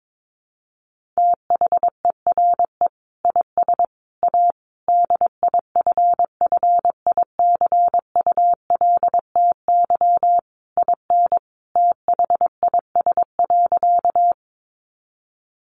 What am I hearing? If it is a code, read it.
Morse code, heard as a keyed tone, about 22 wpm, 700 Hz. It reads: THERE IS A DIFFICULTY IN THIS.
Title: Sentences from Top 1500 Words 22wpm